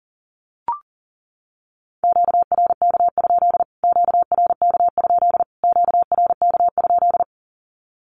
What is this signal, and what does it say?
Morse code, heard as a keyed tone, about 40 wpm, 700 Hz. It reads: QRK? QRK? QRK?
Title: Q-Codes Repeated 3x 40wpm